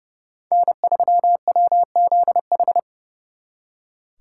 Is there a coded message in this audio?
Yes: Morse code from 0.5 to 2.8 s, heard as a keyed tone, about 39 wpm, 700 Hz. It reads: N3WZH